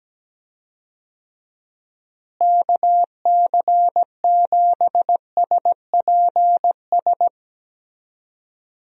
Morse code, heard as a keyed tone, about 17 wpm, 700 Hz. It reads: KC7SPS
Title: USA Call Signs - 6 Characters - Repeated 3x - 17wpm